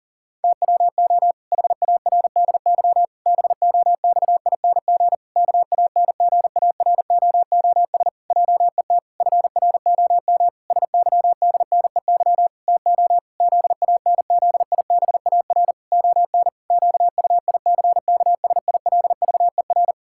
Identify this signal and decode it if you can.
Morse code, heard as a keyed tone, about 37 wpm, 700 Hz. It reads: TWO HARDY BOXING KANGAROOS JET FROM SYDNEY TO ZANZIBAR ON QUICKSILVER
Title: Pangram Challenge v2 - 40wpm